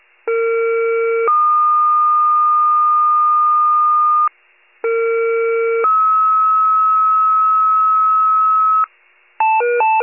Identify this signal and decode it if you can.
Two-tone page, heard as a one-second tone, then a three-second tone, about 460 then 1220 Hz, about 460 then 1290 Hz. It is heard from 0.3 to 8.8 s.
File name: wcalert